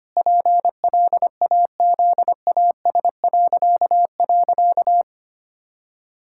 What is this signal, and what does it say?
Morse code, heard as a keyed tone, about 25 wpm, 700 Hz. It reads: PLAZAS..
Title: Pangram Challenge v2 - 25wpm